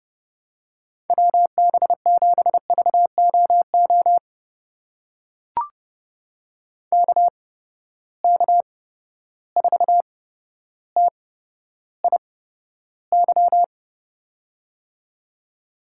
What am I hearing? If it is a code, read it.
Morse code, heard as a keyed tone, about 30 wpm, spaced as for 13 wpm, 700 Hz. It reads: WB7VOO KK4TIY
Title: USA Call Signs - 6 Characters - 8x Character Spacing - 30wpm